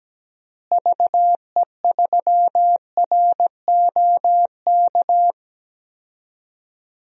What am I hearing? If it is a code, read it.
Morse code, heard as a keyed tone, about 17 wpm, 700 Hz. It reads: VE3ROK
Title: DXCC-100 Call Signs - 6 Characters - Repeated 2x - 17wpm